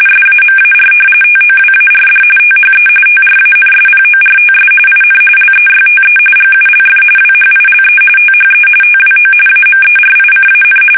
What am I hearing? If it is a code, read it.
RTTY, about 45.45 baud. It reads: SAMAR TOWARD THE TRANSPORTS IN LEYTE GULF WHILE THE TWO AMERICAN
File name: fsk850 / battleshipnj